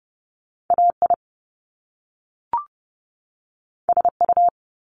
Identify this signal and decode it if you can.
Morse code, heard as a keyed tone, about 30 wpm, 700 Hz. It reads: AI SU
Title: Top Two Letter Combinations in Words 30wpm